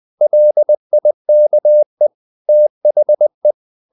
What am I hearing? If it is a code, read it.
Morse code, heard as a keyed tone, about 20 wpm, 600 Hz. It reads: LIKE THE
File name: CW / Practice / EnglishSayings20